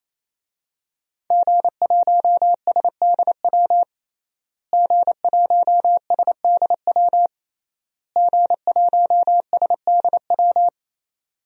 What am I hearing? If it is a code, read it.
Morse code, heard as a keyed tone, about 28 wpm, 700 Hz. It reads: G1SDW G1SDW G1SDW
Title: DXCC-100 Call Signs - 5 Characters - Repeated 3x - 28wpm